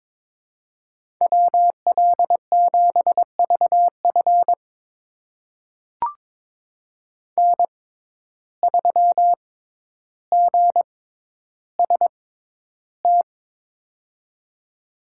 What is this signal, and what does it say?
Morse code, heard as a keyed tone, about 21 wpm, spaced as for 11 wpm, 700 Hz. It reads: WL7VF N3GST
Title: USA Call Signs - 5 Characters - 6x Character Spacing - 22wpm